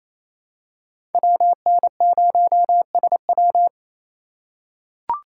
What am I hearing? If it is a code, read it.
Morse code, heard as a keyed tone, about 27 wpm, 700 Hz. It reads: WN0SW
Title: USA Call Signs - 6x Character Spacing - 28wpm